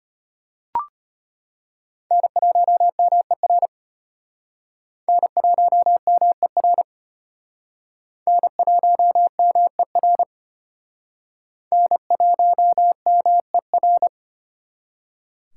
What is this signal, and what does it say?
Morse code, heard as a keyed tone, about 34 wpm, 700 Hz. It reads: N1MER N1MER N1MER N1MER